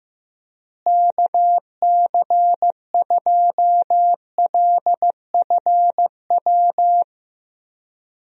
Morse code, heard as a keyed tone, about 15 wpm, 700 Hz. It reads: KC2LFW
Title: USA Call Signs - 6 Characters - Repeated 2x - 15wpm